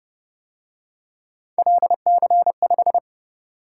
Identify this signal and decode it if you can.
Morse code, heard as a keyed tone, about 28 wpm, 700 Hz. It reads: LC5